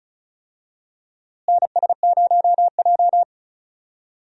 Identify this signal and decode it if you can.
Morse code, heard as a keyed tone, about 36 wpm, 700 Hz. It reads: NS0J
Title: USA Call Signs - 4 Characters - Repeated 2x - 35wpm